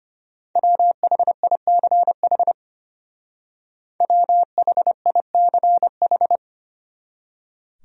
Morse code, heard as a keyed tone, about 33 wpm, 700 Hz. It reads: WHICH WHICH